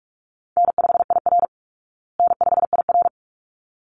Morse code, heard as a keyed tone, about 44 wpm, 700 Hz. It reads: N5IR N5IR
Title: Easy Call Signs Repeated 2x - 45wpm